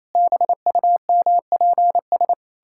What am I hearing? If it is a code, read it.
Morse code, heard as a keyed tone, about 28 wpm, 700 Hz. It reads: BUMPS
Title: Pangram Challenge v2 - 28wpm